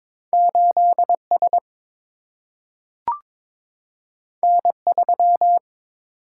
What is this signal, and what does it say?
Morse code, heard as a keyed tone, about 22 wpm, 700 Hz. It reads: 8S N3
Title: Single-Letter Single-Number 22wpm